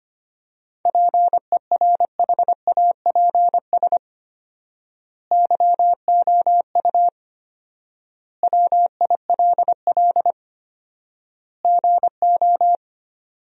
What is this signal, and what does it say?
Morse code, heard as a keyed tone, about 25 wpm, 700 Hz. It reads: PERHAPS YOU WILL GO
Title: 4x Word-Spacing - Sentences from Top 500 Words - 25wpm